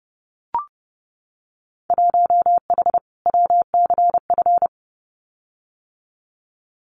Morse code, heard as a keyed tone, about 30 wpm, 700 Hz. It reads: 1H WCF
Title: Winter Field Day Exchanges 30wpm